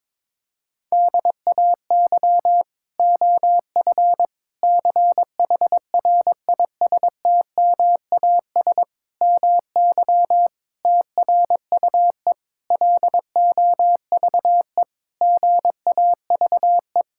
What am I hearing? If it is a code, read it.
Morse code, heard as a keyed tone, about 22 wpm, 700 Hz. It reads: DAY OF CHRISTMAS MY TRUE LOVE GAVE
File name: Sets of 9 Words - 22wpm